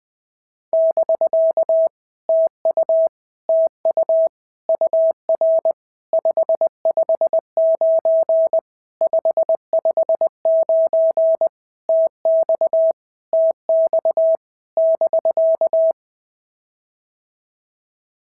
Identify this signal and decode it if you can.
Morse code, heard as a keyed tone, about 20 wpm, 650 Hz. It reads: <BK> TU TU UR 559 559 TX TX <BK>
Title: POTA QSO Practice - 20wpm